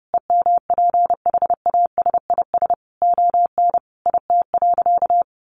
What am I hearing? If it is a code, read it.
Morse code, heard as a keyed tone, about 30 wpm, 700 Hz. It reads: EMPHASIS ON IT.
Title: Sentences from Top 1400 Words 30wpm